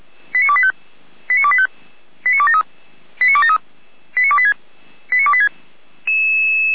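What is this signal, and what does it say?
Five-tone selective call (ZVEI), heard as quick runs of five tones, tones of about 70 ms. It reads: 89285 89285 89273 89273 89286 89286
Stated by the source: ZVEI five-tone selective call received on an SDR.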